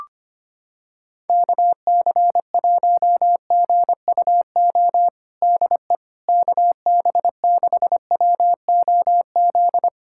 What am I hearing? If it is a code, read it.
Morse code, heard as a keyed tone, about 25 wpm, 700 Hz. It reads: KC1GUO DE KB6WOZ
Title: USA Callsign DE Callsign 25wpm